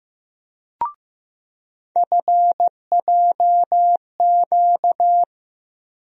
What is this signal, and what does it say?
Morse code, heard as a keyed tone, about 16 wpm, 700 Hz. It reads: FJQ
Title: USA Call Sign Postfixes - 3 Letters 15wpm